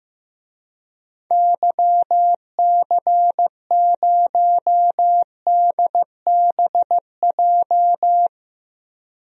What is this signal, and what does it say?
Morse code, heard as a keyed tone, about 15 wpm, 700 Hz. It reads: YC0DBJ
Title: DXCC-100 Call Signs - 6 Characters - Repeated 2x - 15wpm